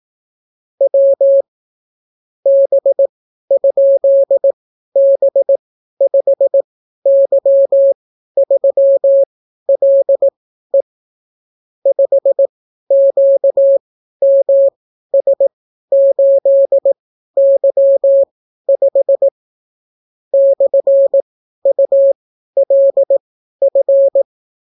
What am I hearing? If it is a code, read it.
Morse code, heard as a keyed tone, about 18 wpm, spaced as for 12 wpm, 550 Hz. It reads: W B?B5Y3LE 5QMS8Y5 /ULF